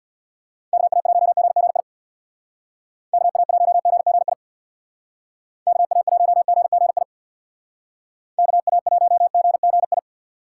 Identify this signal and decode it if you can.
Morse code, heard as a keyed tone, about 58 wpm, 700 Hz. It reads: KA1GGI KA1GGI KA1GGI KA1GGI
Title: Speed Racing - USA 2x3 Call Signs 50wpm